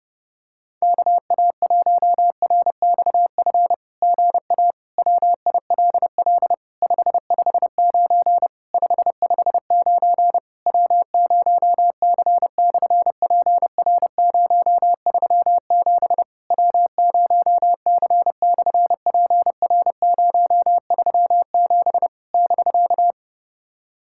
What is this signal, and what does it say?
Morse code, heard as a keyed tone, about 30 wpm, 700 Hz. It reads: KA1RXF GA WILL 559 559 W0C/PR037 W0C/PR037 <BK>
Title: SOTA QSO Practice 30wpm